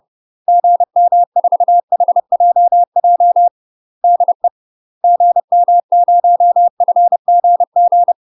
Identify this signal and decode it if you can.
Morse code, heard as a keyed tone, about 30 wpm, 700 Hz. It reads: GM4HJJ DE GM0FGG
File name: UK MASTER QSOs TRAINING / 30WPM / UK MASTER QSOs 0008